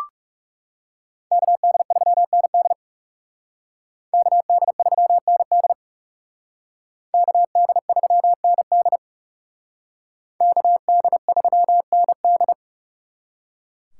Morse code, heard as a keyed tone, about 36 wpm, 700 Hz. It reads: KD3ND KD3ND KD3ND KD3ND
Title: Speed Racing - USA 2x2 Call Signs 30wpm